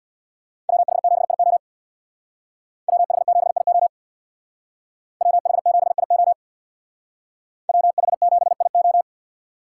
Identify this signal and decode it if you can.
Morse code, heard as a keyed tone, about 63 wpm, 690 Hz. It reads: WH7IO WH7IO WH7IO WH7IO